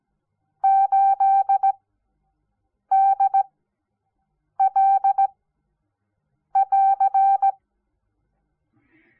Morse code, heard as a keyed tone, about 17 wpm, 780 Hz. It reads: 8 D L <AR>
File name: Onlinelehrgang / morsen / DJ4UF Morsen06